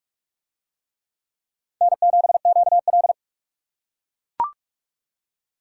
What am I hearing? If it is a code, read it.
Morse code, heard as a keyed tone, about 42 wpm, 700 Hz. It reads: N7QL